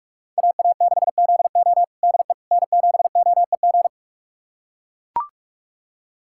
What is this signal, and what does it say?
Morse code, heard as a keyed tone, about 44 wpm, 700 Hz. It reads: AA6ZO DE N7OEG
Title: USA Callsign DE Callsign - 5 Characters - 45wpm